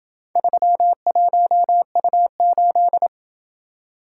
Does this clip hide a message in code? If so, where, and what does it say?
Yes: Morse code from 0.4 to 3.1 s, heard as a keyed tone, about 27 wpm, 700 Hz. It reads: 31U8